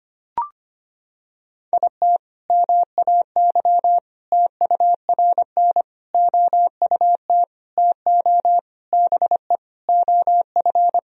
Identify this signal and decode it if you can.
Morse code, heard as a keyed tone, about 25 wpm, 700 Hz. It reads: IT MAY TURN OUT TO BE OF
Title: Sentences from Top 400 Words and Abbreviations 25wpm